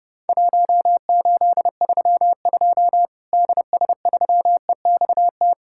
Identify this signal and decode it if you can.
Morse code, heard as a keyed tone, about 32 wpm, 700 Hz. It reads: 1832 DS3EXT
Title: Worked All Europe DX Contest - QTC Exchange - 30wpm